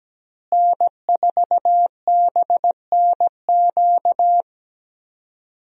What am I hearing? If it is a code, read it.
Morse code, heard as a keyed tone, about 17 wpm, 700 Hz. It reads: N4BNQ
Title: USA Call Signs - 5 Characters - Repeated 2x - 17wpm